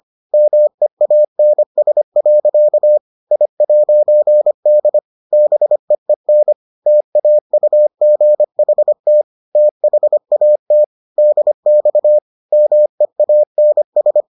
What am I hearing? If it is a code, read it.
Morse code, heard as a keyed tone, about 25 wpm, 600 Hz. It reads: MEANS. I'D BEEN TAUGHT THAT DX MEANS